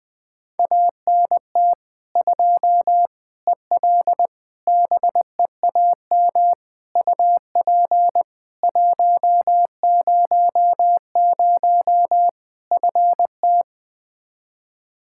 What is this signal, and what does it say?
Morse code, heard as a keyed tone, about 20 wpm, 700 Hz. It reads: ANT 2 EL BEAM UP 100 FT
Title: K1USN SST QSO Practice 20wpm